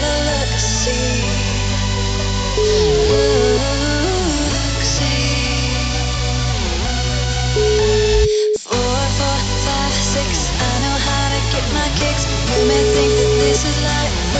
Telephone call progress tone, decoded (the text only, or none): ringback tone